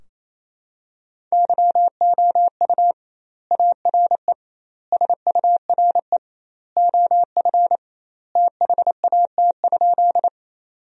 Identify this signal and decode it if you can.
Morse code, heard as a keyed tone, about 27 wpm, 700 Hz. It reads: YOU ARE SURE OF THAT?
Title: 2x Word-Spacing - Sentences from Top 300 Words - 28wpm